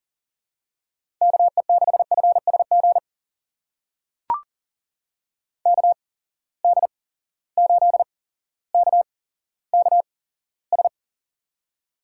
Morse code, heard as a keyed tone, about 38 wpm, spaced as for 17 wpm, 700 Hz. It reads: KE6FSG KD8KKS
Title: USA Call Signs - 8x Character Spacing - 40wpm